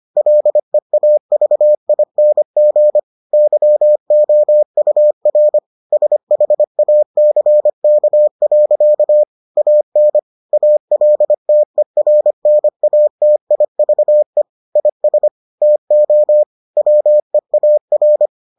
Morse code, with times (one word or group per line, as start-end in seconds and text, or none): 0.2-3.0 LEAVING
3.3-5.6 YOUR
5.9-9.2 SHACK.
9.6-10.2 AN
10.5-14.4 ALTERNATIVE
14.8-15.3 IS
15.6-16.4 TO
16.8-18.3 WEAR